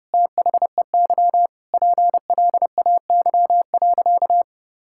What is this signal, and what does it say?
Morse code, heard as a keyed tone, about 30 wpm, 700 Hz. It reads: THEY PLAY.
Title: Sentences from Top 1000 Words - Review Part 5 - 30wpm